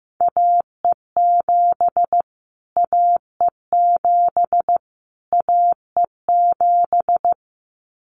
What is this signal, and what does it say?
Morse code, heard as a keyed tone, about 15 wpm, 700 Hz. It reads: AE7 AE7 AE7
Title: Callsign Prefixes for Top 25 DXCC - Repeated 3x - 15wpm